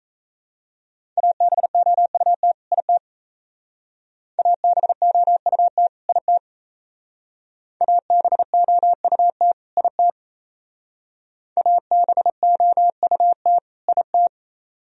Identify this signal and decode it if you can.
Morse code, heard as a keyed tone, about 35 wpm, 700 Hz. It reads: ABOUT IT ABOUT IT ABOUT IT ABOUT IT